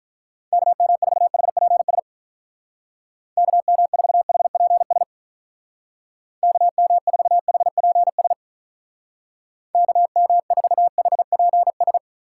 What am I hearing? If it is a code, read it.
Morse code, heard as a keyed tone, about 43 wpm, 700 Hz. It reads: KM4HPS KM4HPS KM4HPS KM4HPS
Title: Speed Racing - USA 2x3 Call Signs 35wpm